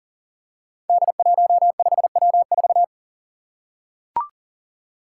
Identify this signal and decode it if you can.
Morse code, heard as a keyed tone, about 40 wpm, 700 Hz. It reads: D15W4
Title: Five Challenging Letter-Number Combinations 40wpm